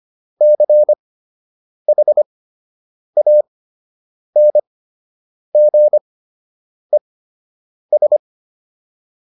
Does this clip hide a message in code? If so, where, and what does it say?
Yes: Morse code from 0.4 to 8.2 s, heard as a keyed tone, about 25 wpm, spaced as for 8 wpm, 600 Hz. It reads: CHANGES